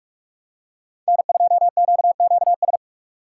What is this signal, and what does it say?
Morse code, heard as a keyed tone, about 45 wpm, 700 Hz. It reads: N2QQS
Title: USA Call Signs - 5 Characters - 4x Character Spacing - 45wpm